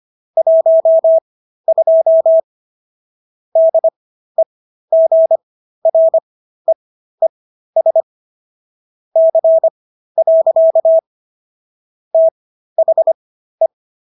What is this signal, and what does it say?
Morse code, heard as a keyed tone, about 25 wpm, spaced as for 13 wpm, 650 Hz. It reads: 12 DEGREES C. THE